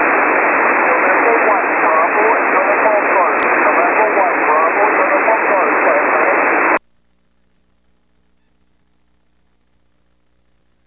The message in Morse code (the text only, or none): none